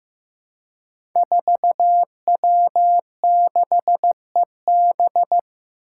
Morse code, heard as a keyed tone, about 15 wpm, 700 Hz. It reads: 4W6EB
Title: Five Challenging Letter-Number Combinations 15wpm